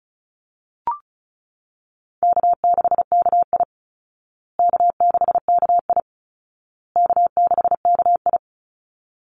Morse code, heard as a keyed tone, about 34 wpm, 700 Hz. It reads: K6KI K6KI K6KI
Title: Call Signs Repeated 2x - 35wpm